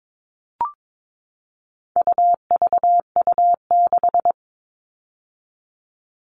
Morse code, heard as a keyed tone, about 22 wpm, 700 Hz. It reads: UVU6
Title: All Cut Numbers - 4-Digit 22wpm